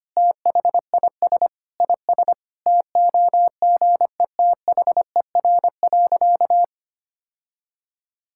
Morse code, heard as a keyed tone, about 25 wpm, 700 Hz. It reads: THIS IS TOGETHER.
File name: practice / 25wpm / Sentences with 4-Word Phrases - 25wpm